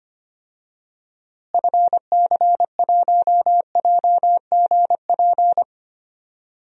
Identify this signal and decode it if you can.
Morse code, heard as a keyed tone, about 25 wpm, 700 Hz. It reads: FC1JGP
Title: DXCC-100 Call Signs - 6 Characters - Repeated 3x - 25wpm